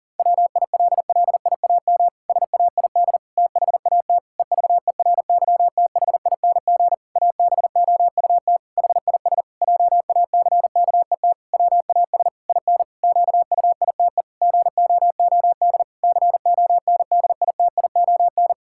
Morse code, heard as a keyed tone, about 40 wpm, 700 Hz. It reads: WILLIAM SAID THAT EVERYTHING ABOUT HIS JACKET WAS IN QUITE GOOD CONDITION
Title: Pangram Challenge v2 - 40wpm